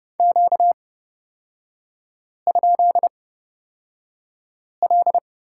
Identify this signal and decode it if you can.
Morse code, heard as a keyed tone, about 30 wpm, spaced as for 5 wpm, 700 Hz. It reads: Q?L
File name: Lesson 38e - Letter q - Three Characters 30wpm